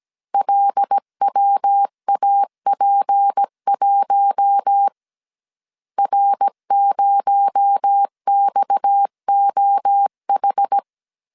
Morse code, heard as a keyed tone, about 17 wpm, 800 Hz. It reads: LWAP1 R0XOH
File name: CW / DET / 1 9 1500C S C 16wpm 2909